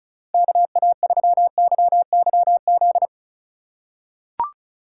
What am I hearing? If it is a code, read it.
Morse code, heard as a keyed tone, about 36 wpm, 700 Hz. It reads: KA3YYZ
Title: USA Call Signs - 6 Characters - 4x Character Spacing - 35wpm